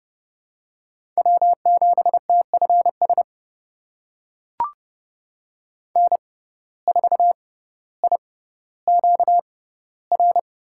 Morse code, heard as a keyed tone, about 30 wpm, spaced as for 15 wpm, 700 Hz. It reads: W7TFS N4IQR
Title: USA Call Signs - 6x Character Spacing - 30wpm